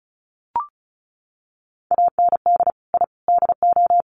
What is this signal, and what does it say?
Morse code, heard as a keyed tone, about 34 wpm, 700 Hz. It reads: AND I DO
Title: Sets of 3 Words 35wpm